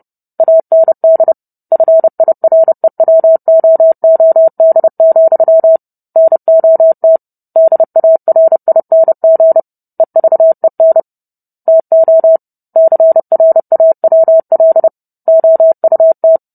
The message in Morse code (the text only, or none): AND FIREWOOD, NOT DARING EVEN TO CRAWL OUT